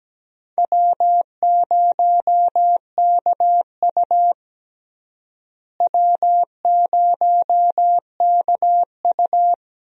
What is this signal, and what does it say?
Morse code, heard as a keyed tone, about 17 wpm, 700 Hz. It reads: W0KU W0KU
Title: USA Call Signs - 4 Characters - Repeated 2x - 17wpm